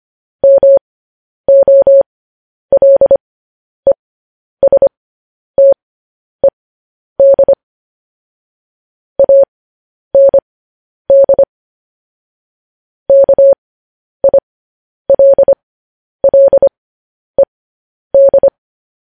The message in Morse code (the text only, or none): MOLESTED AND KILLED